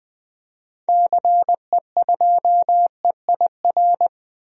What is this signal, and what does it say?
Morse code, heard as a keyed tone, about 20 wpm, 700 Hz. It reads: CE2EIR
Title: DXCC-100 Call Signs - 6 Characters - Repeated 3x - 20wpm